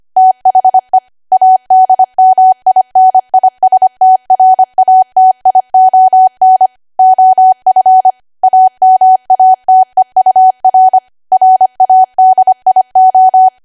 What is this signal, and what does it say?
Morse code, heard as a keyed tone, about 25 wpm, 750 Hz. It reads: THE ADMINISTRATION OF AMATEUR RADIO